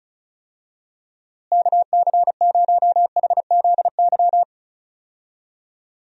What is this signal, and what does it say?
Morse code, heard as a keyed tone, about 34 wpm, 700 Hz. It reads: KC0HZY
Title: USA Call Signs - 6 Characters - Repeated 2x - 35wpm